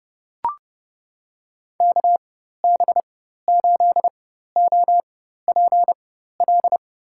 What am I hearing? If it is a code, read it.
Morse code, heard as a keyed tone, about 30 wpm, spaced as for 14 wpm, 700 Hz. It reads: KB8OPL
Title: USA Call Signs - 6 Characters - 4x Character Spacing - 30wpm